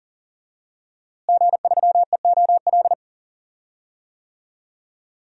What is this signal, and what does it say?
Morse code, heard as a keyed tone, about 40 wpm, 700 Hz. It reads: G3EOL
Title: DXCC-100 Call Signs - 5 Characters - Repeated 3x - 40wpm